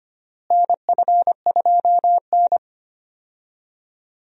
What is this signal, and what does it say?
Morse code, heard as a keyed tone, about 25 wpm, 700 Hz. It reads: NF2N
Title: USA Call Signs - 4 Characters - Repeated 3x - 25wpm